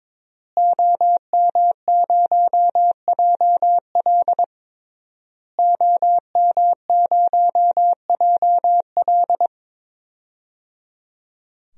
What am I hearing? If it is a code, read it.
Morse code, heard as a keyed tone, about 22 wpm, 700 Hz. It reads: OM0JL OM0JL
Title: DXCC-100 Call Signs - 5 Characters - Repeated 3x - 22wpm